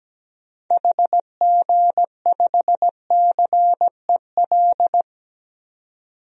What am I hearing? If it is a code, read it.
Morse code, heard as a keyed tone, about 17 wpm, 700 Hz. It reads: HG5CEL
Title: DXCC-100 Call Signs - 6 Characters - 17wpm